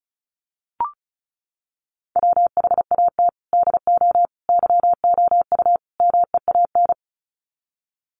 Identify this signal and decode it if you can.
Morse code, heard as a keyed tone, about 34 wpm, 700 Hz. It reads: WHAT DO YOU MEAN
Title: Sets of 4 Words 35wpm